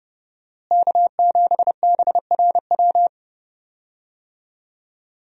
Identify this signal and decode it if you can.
Morse code, heard as a keyed tone, about 30 wpm, 700 Hz. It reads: K7BRW
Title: USA Call Signs - 5 Characters - Repeated 3x - 30wpm